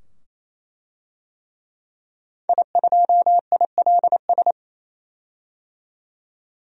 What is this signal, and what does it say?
Morse code, heard as a keyed tone, about 28 wpm, 700 Hz. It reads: I2ILS
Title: DXCC-100 Call Signs - 5 Characters - 28wpm